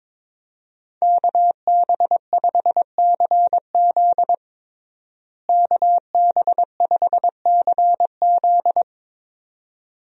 Morse code, heard as a keyed tone, about 22 wpm, 700 Hz. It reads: KB5CZ KB5CZ
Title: USA Call Signs - 5 Characters - Repeated 2x - 22wpm